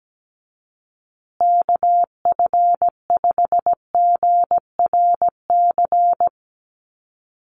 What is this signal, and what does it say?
Morse code, heard as a keyed tone, about 17 wpm, 700 Hz. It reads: KF5GRC
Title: USA Call Signs Repeated 2x 17wpm